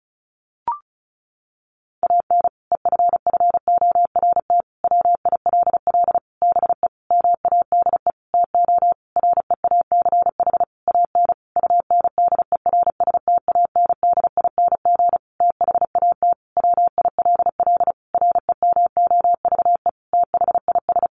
Morse code, heard as a keyed tone, about 34 wpm, 700 Hz. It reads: AN EFFORT WILL BE MADE TO REACH AN UNDERSTANDING THAT WILL REMOVE THIS
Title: Sentences from Top 1500 Words 35wpm